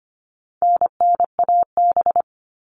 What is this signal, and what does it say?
Morse code, heard as a keyed tone, about 25 wpm, 700 Hz. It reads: NNAB
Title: All Cut Numbers - 4-Digit 25wpm